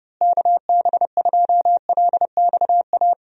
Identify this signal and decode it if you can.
Morse code, heard as a keyed tone, about 30 wpm, 700 Hz. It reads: KB2LXA
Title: USA Call Signs - 6 Characters - Repeated 3x - 30wpm